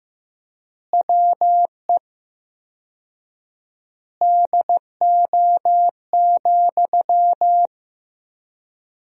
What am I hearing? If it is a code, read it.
Morse code, heard as a keyed tone, about 15 wpm, 700 Hz. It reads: WE DO,
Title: 4x Word-Spacing - Sentences from Top 100 Words - 15wpm